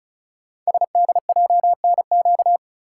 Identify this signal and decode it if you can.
Morse code, heard as a keyed tone, about 36 wpm, 700 Hz. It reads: SDJNQ
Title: Code Groups - 5 Letter:Numbers - 35wpm